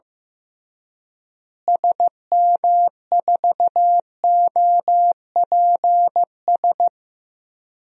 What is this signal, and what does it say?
Morse code, heard as a keyed tone, about 15 wpm, 700 Hz. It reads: SM4OPS
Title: DXCC-100 Call Signs - 6 Characters - Repeated 3x - 15wpm